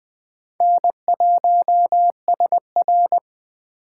Morse code, heard as a keyed tone, about 21 wpm, 700 Hz. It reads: N1SR